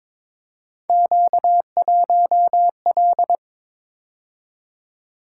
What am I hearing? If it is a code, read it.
Morse code, heard as a keyed tone, about 22 wpm, 700 Hz. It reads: Q1L
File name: practice / 22wpm / Three Letter-Number Combinations - 22wpm